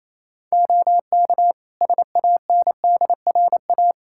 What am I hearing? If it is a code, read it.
Morse code, heard as a keyed tone, about 28 wpm, 700 Hz. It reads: OK SANDRA
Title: Practice QSO - Exchange 3 - More Info - 28wpm